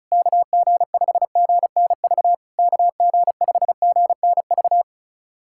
Morse code, heard as a keyed tone, about 36 wpm, 690 Hz. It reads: KG5GNV KG5GNV
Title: SOTA QSO Practice 35wpm